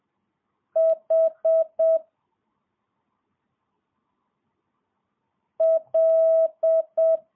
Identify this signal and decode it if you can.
Morse code, heard as a keyed tone, about 7 wpm, 650 Hz. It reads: H L
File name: jh8cbh / 2022cw